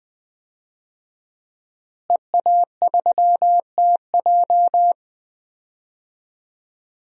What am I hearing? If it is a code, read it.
Morse code, heard as a keyed tone, about 20 wpm, 700 Hz. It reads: EA3TJ
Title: DXCC-100 Call Signs - 5 Characters - Repeated 3x - 20wpm